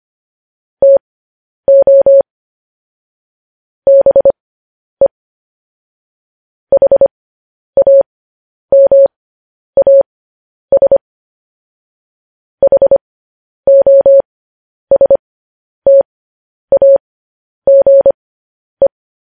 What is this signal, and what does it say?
Morse code, heard as a keyed tone, about 25 wpm, spaced as for 10 wpm, 550 Hz. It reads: TO BE HAMAS HOSTAGE